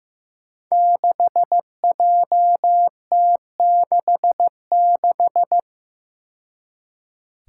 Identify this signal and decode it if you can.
Morse code, heard as a keyed tone, about 15 wpm, 700 Hz. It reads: 6JT66
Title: Five Challenging Letter-Number Combinations 15wpm